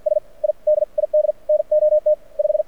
Morse code, heard as a keyed tone, about 50 wpm, spaced as for 43 wpm, 600 Hz. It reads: S I DID NOT 5
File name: archived / CRSnet